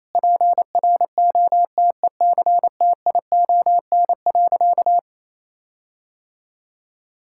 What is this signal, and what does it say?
Morse code, heard as a keyed tone, about 27 wpm, 700 Hz. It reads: PROTECTION.
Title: Sentences from Top 1600 Words 28wpm